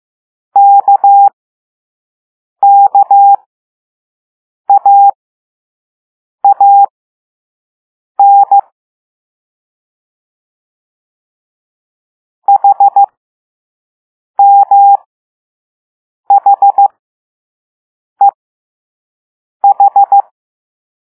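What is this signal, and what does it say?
Morse code, heard as a keyed tone, about 15 wpm, spaced as for 5 wpm, 800 Hz. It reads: KKAAN HMHEH
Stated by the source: Morse code lesson 5 drill of random five-letter groups using all letters learned so far.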